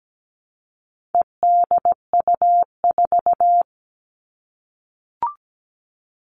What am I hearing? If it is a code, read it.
Morse code, heard as a keyed tone, about 17 wpm, 700 Hz. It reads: EDU4
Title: All Cut Numbers - 4-Digit 17wpm